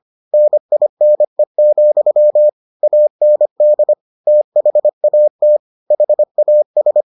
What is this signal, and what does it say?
Morse code, heard as a keyed tone, about 25 wpm, 600 Hz. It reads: NINE, AND THAT HAS